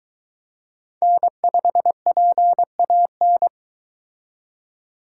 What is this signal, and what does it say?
Morse code, heard as a keyed tone, about 23 wpm, 700 Hz. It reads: N5PAN